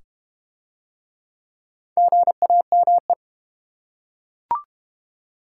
Morse code, heard as a keyed tone, about 32 wpm, 700 Hz. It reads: GAME